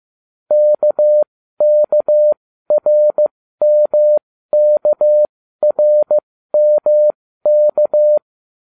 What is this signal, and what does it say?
Morse code, heard as a keyed tone, about 15 wpm, spaced as for 13 wpm, 600 Hz. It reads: KKRMKRMK